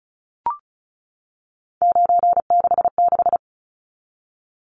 Morse code, heard as a keyed tone, about 34 wpm, 700 Hz. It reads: 966
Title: Numbers - 3-Digit 35wpm